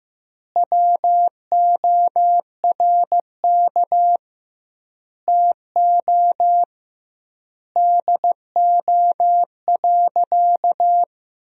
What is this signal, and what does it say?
Morse code, heard as a keyed tone, about 15 wpm, 700 Hz. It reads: WORK TO DO.
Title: 2x Word-Spacing - Sentences from Top 200 Words - 15wpm